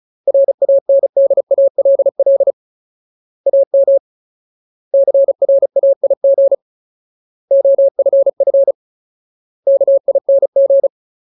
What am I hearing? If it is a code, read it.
Morse code, heard as a keyed tone, about 34 wpm, 550 Hz. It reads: RANDALL AM CRAIG OFF KING
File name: Ultimate Word List / 35WPM Ultimate